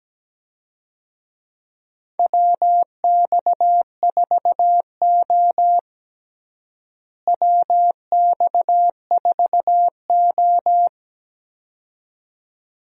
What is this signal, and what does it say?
Morse code, heard as a keyed tone, about 17 wpm, 700 Hz. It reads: WX4O WX4O
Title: USA Call Signs - 4 Characters - Repeated 2x - 17wpm